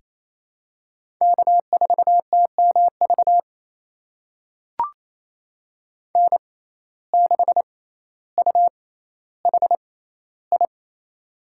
Morse code, heard as a keyed tone, about 28 wpm, spaced as for 14 wpm, 700 Hz. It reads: K4TMV N6UHI
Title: USA Call Signs - 5 Characters - 6x Character Spacing - 28wpm